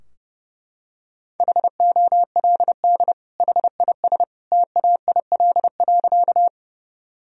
Morse code, heard as a keyed tone, about 30 wpm, 700 Hz. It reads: HOLD HIS TAIL.
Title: Sentences from Top 1000 Words - Review Part 3 - 30wpm